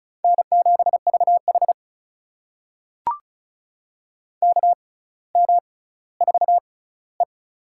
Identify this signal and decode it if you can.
Morse code, heard as a keyed tone, about 36 wpm, spaced as for 18 wpm, 700 Hz. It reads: N7VH KM4E
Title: USA Call Signs - 4 Characters - 6x Character Spacing - 35wpm